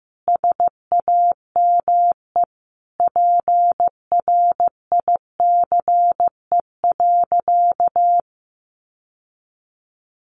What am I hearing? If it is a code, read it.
Morse code, heard as a keyed tone, about 15 wpm, 700 Hz. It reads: SAME PRICE.
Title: Sentences from Top 600 Words 15wpm